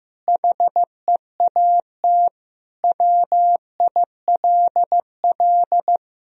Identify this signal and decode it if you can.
Morse code, heard as a keyed tone, about 15 wpm, 700 Hz. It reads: HEAT WILL
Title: Sentences from Top 1000 Words - Review Part 3 - 15wpm